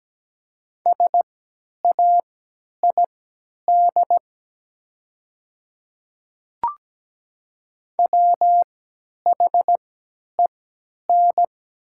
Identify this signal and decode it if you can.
Morse code, heard as a keyed tone, about 17 wpm, spaced as for 10 wpm, 700 Hz. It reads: SAID WHEN
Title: 3x Character Spacing - Top 100 Words - 17wpm